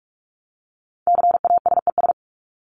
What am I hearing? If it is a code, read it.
Morse code, heard as a keyed tone, about 44 wpm, 700 Hz. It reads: CASES
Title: ES:S Suffix - Top 100 Words with Base Words 45wpm